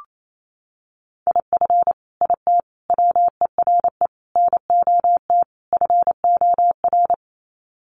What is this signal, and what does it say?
Morse code, heard as a keyed tone, about 28 wpm, 700 Hz. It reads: IF IT WERE NOT FOR
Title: Sets of 5 Words 28wpm